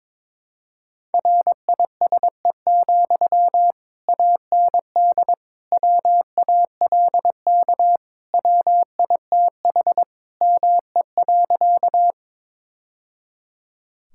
Morse code, heard as a keyed tone, about 22 wpm, 700 Hz. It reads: RISE, AND WALK WITH ME.
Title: Sentences from Top 1000 Words - Review Part 6 - 22wpm